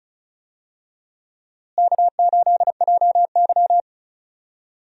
Morse code, heard as a keyed tone, about 36 wpm, 700 Hz. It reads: K8JY
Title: USA Call Signs - 4 Characters - Repeated 2x - 35wpm